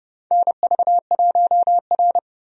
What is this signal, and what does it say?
Morse code, heard as a keyed tone, about 32 wpm, 700 Hz. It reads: NV1R
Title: USA Call Signs - 4 Characters - 30wpm